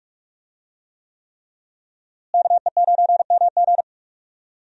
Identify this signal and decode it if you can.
Morse code, heard as a keyed tone, about 47 wpm, 700 Hz. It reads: KE9MG
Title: USA Call Signs - 5 Characters - Repeated 3x - 45wpm